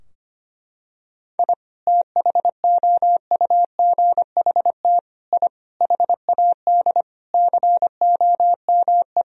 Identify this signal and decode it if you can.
Morse code, heard as a keyed tone, about 25 wpm, 700 Hz. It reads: I THOUGHT I HAD COME
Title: Sentences from Top 1000 Words - Review Part 4 - 25wpm